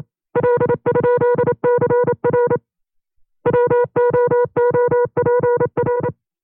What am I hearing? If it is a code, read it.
Morse code, heard as a keyed tone, about 27 wpm, 480 Hz. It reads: L?CR WOOPR
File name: FreeFiles / CW-practice / 25WPM-side1